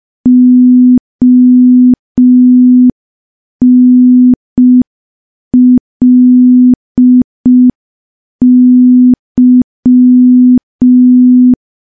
Morse code, heard as a keyed tone, about 5 wpm, 250 Hz. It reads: ONLY
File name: morse / morse-training-05WPM